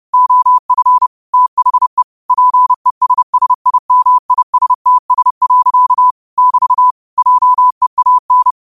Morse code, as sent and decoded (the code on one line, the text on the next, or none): --- ..-. / - .... / .--. ... ... .. -- .. ... - ... .-.-.- / -...- / .--- . .- -.
OF THE PESSIMISTS. <BT> JEAN